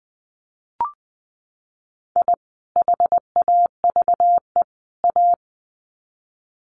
Morse code, heard as a keyed tone, about 20 wpm, 700 Hz. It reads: I HAVE A
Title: Sets of 3 Words 20wpm